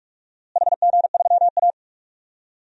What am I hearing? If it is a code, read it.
Morse code, heard as a keyed tone, about 46 wpm, 700 Hz. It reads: HG3A
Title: DXCC-100 Call Signs - 4 Characters - Repeated 2x - 45wpm